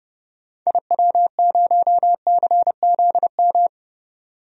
Morse code, heard as a keyed tone, about 31 wpm, 700 Hz. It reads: IW0CZM
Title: DXCC-100 Call Signs - 6 Characters - 30wpm